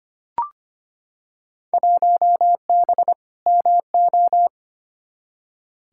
Morse code, heard as a keyed tone, about 25 wpm, 700 Hz. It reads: 1B MO
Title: Field Day Exchanges - 25wpm